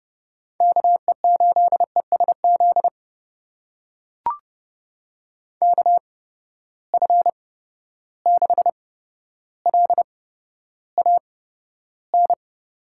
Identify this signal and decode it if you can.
Morse code, heard as a keyed tone, about 30 wpm, spaced as for 13 wpm, 700 Hz. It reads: KE8ESZ KF6LAN
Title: USA Call Signs - 6 Characters - 8x Character Spacing - 30wpm